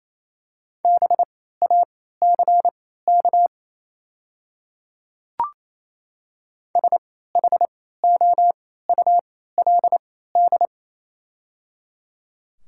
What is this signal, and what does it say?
Morse code, heard as a keyed tone, about 28 wpm, spaced as for 16 wpm, 690 Hz. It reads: BACK SHOULD
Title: 3x Character Spacing - Top 500 Words - 28wpm